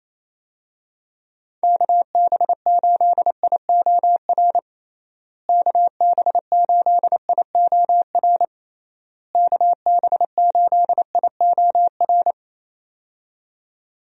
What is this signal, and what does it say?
Morse code, heard as a keyed tone, about 28 wpm, 700 Hz. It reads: KB8IOR KB8IOR KB8IOR
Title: USA Call Signs - 6 Characters - Repeated 3x - 28wpm